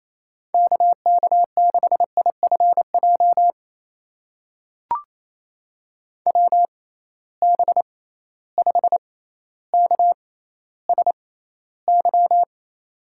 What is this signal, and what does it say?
Morse code, heard as a keyed tone, about 28 wpm, spaced as for 14 wpm, 700 Hz. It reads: KK6IFJ WB5KSY
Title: USA Call Signs - 6 Characters - 6x Character Spacing - 28wpm